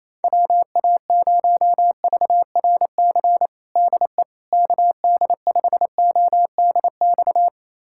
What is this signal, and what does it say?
Morse code, heard as a keyed tone, about 28 wpm, 700 Hz. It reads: WA0VRC DE KD5ODX
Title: USA Callsign DE Callsign - 6 Characters - 28wpm